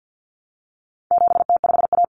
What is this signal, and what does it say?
Morse code, heard as a keyed tone, about 50 wpm, 700 Hz. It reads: 7T5A
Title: Common Cut Numbers - 4-Digit 50wpm